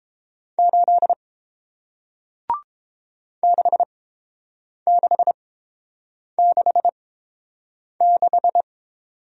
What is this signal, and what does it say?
Morse code, heard as a keyed tone, about 31 wpm, spaced as for 7 wpm, 700 Hz. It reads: 8 6666